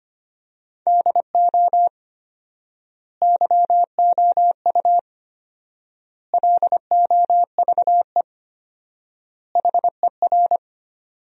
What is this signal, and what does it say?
Morse code, heard as a keyed tone, about 25 wpm, 700 Hz. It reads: DO YOU LOVE HER
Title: 4x Word-Spacing - Sentences from Top 400 Words - 25wpm